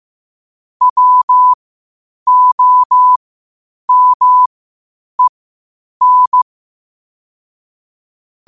Morse code, heard as a keyed tone, about 15 wpm, spaced as for 8 wpm, 1000 Hz. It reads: WOMEN
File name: qotdfiles / 8wpm